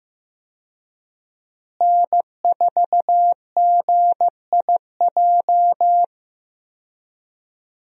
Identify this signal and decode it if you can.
Morse code, heard as a keyed tone, about 16 wpm, 700 Hz. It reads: N4GIJ
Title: USA Call Signs - 5 Characters - Repeated 2x - 15wpm